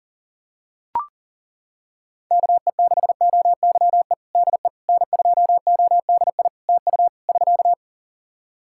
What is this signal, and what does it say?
Morse code, heard as a keyed tone, about 37 wpm, 700 Hz. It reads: KE6OYE DE N2ODI TU <SK>
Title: K1USN SST QSO Practice 40wpm